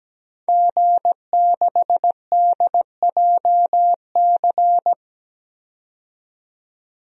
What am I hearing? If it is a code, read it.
Morse code, heard as a keyed tone, about 17 wpm, 700 Hz. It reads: G6DJC
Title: DXCC-100 Call Signs - 5 Characters - 17wpm